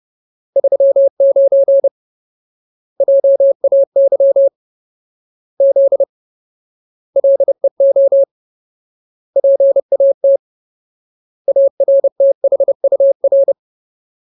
Morse code, heard as a keyed tone, about 30 wpm, 550 Hz. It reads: 39 JAY Z LEO PAT ARTHUR
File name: Ultimate Word List / 30WPM Ultimate